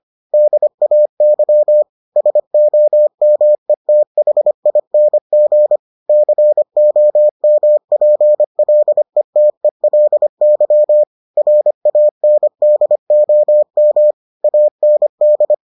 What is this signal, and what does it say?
Morse code, heard as a keyed tone, about 25 wpm, 600 Hz. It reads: DAY SOMETHING COMPLETELY RANDOM AND